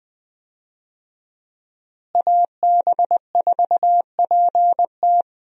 Morse code, heard as a keyed tone, about 20 wpm, 700 Hz. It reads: AB4PT
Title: USA Call Signs - 5 Characters - Repeated 3x - 20wpm